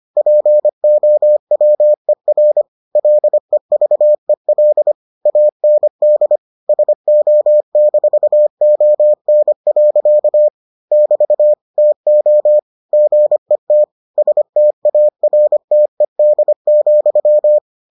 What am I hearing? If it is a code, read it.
Morse code, heard as a keyed tone, about 25 wpm, 600 Hz. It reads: POWER LEVEL AND SO-ON. <BT> TO GET STARTED,